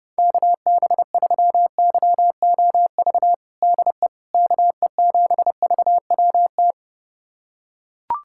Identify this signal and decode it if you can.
Morse code, heard as a keyed tone, about 30 wpm, 700 Hz. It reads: KB3YOV DE KE7VWT
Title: USA Callsign DE Callsign - 6 Characters - 30wpm